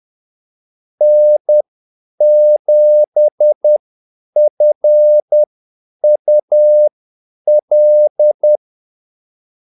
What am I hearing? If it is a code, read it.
Morse code, heard as a keyed tone, about 10 wpm, spaced as for 8 wpm, 600 Hz. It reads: N7FUL
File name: Callsigns-10-8wpm